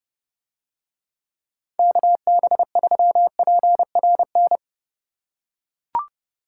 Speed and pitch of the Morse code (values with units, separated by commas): 30 wpm, 700 Hz